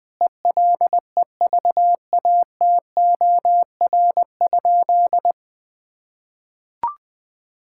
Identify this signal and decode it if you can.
Morse code, heard as a keyed tone, about 20 wpm, 700 Hz. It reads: ELEVATOR?
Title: Sentences from Top 1600 Words 20wpm